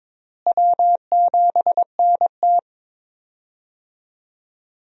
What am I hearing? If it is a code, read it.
Morse code, heard as a keyed tone, about 22 wpm, 700 Hz. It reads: W7NT
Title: USA Call Signs - 4 Characters - Repeated 3x - 22wpm